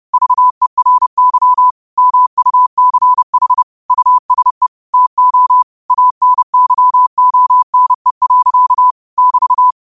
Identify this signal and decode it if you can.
Morse code, heard as a keyed tone, about 30 wpm, 1000 Hz. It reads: VERY MUCH USE TO ANYONE. <BT>
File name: qotdfiles / 30wpm